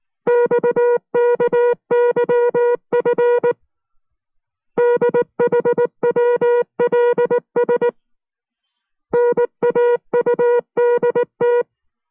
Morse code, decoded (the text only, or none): XKYF DHWLS NAUDT